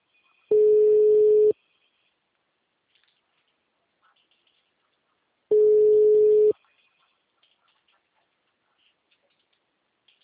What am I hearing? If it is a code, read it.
Ringback tone (Europe, 425 Hz).
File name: out- -2010- - - .30398